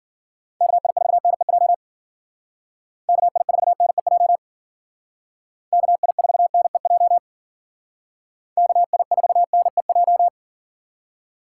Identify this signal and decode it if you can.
Morse code, heard as a keyed tone, about 48 wpm, 700 Hz. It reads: KI4NEJ KI4NEJ KI4NEJ KI4NEJ
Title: Speed Racing - USA Call Signs 40wpm